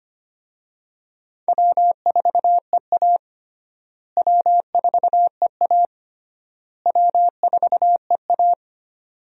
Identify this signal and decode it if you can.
Morse code, heard as a keyed tone, about 25 wpm, 700 Hz. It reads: W4EA W4EA W4EA
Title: USA Call Signs - 4 Characters - Repeated 3x - 25wpm